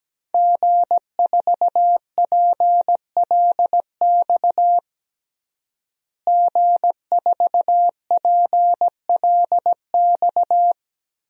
Morse code, heard as a keyed tone, about 17 wpm, 700 Hz. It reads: G4PLX G4PLX
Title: DXCC-100 Call Signs - 5 Characters - Repeated 3x - 17wpm